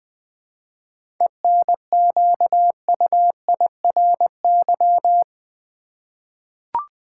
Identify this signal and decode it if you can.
Morse code, heard as a keyed tone, about 20 wpm, 700 Hz. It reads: ENQUIRY
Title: 7 Letter Words - First 3 Letters Given - 20wpm